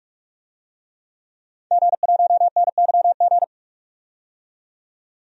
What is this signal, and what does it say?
Morse code, heard as a keyed tone, about 46 wpm, 700 Hz. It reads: G1NYG
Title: DXCC-100 Call Signs - 5 Characters - 45wpm